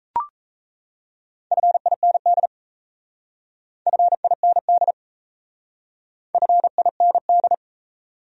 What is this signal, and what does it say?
Morse code, heard as a keyed tone, about 36 wpm, 700 Hz. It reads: FIND FIND FIND